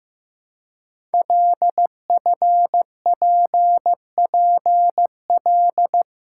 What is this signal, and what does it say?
Morse code, heard as a keyed tone, about 15 wpm, 700 Hz. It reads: LFPPL
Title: Code Groups - 5 Letters - 15wpm